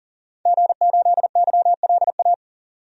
Morse code, heard as a keyed tone, about 40 wpm, 700 Hz. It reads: G8YLA
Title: DXCC-100 Call Signs - 5 Characters - 40wpm